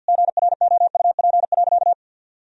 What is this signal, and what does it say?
Morse code, heard as a keyed tone, about 52 wpm, 700 Hz. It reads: GROUP.
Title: Sentences from Top 1000 Words - Review Part 1 - 50wpm